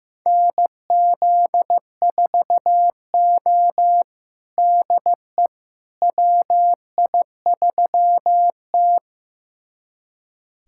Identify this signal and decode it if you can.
Morse code, heard as a keyed tone, about 15 wpm, 700 Hz. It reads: NZ4O DE WI3T
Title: USA Callsign DE Callsign - 4 Characters - 15wpm